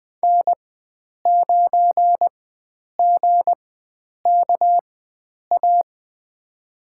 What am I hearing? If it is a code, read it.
Morse code, heard as a keyed tone, about 20 wpm, spaced as for 9 wpm, 700 Hz. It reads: N9GKA
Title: USA Call Signs - 5 Characters - 4x Character Spacing - 20wpm